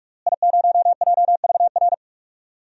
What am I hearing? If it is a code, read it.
Morse code, heard as a keyed tone, about 46 wpm, 700 Hz. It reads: I0JVR
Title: DXCC-100 Call Signs - 5 Characters - Repeated 3x - 45wpm